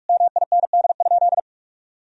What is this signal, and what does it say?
Morse code, heard as a keyed tone, about 41 wpm, 700 Hz. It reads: MIND?
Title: Sentences from Top 1000 Words - Review Part 4 - 45wpm